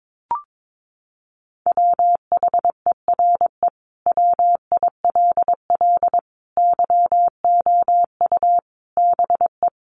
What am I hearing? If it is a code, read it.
Morse code, heard as a keyed tone, about 22 wpm, 700 Hz. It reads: WHERE WILL YOU BE
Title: Sentences from Top 900 Words 22wpm